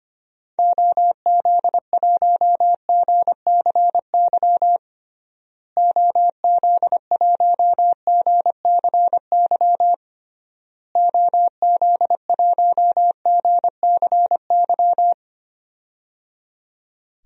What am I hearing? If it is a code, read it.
Morse code, heard as a keyed tone, about 25 wpm, 700 Hz. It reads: OZ1GCY OZ1GCY OZ1GCY
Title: DXCC-100 Call Signs - 6 Characters - Repeated 3x - 25wpm